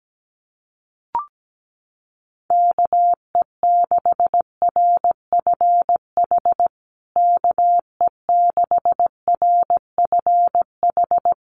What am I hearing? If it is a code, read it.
Morse code, heard as a keyed tone, about 17 wpm, 700 Hz. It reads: KE6RFH KE6RFH
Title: USA Call Signs Repeated 3x 17wpm